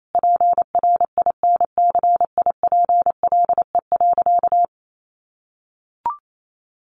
Morse code, heard as a keyed tone, about 28 wpm, 700 Hz. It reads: PRINCIPLE.
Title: Sentences from Top 1400 Words 28wpm